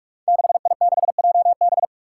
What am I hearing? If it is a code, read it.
Morse code, heard as a keyed tone, about 46 wpm, 700 Hz. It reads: 6I6JB
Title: Five Challenging Letter-Number Combinations 45wpm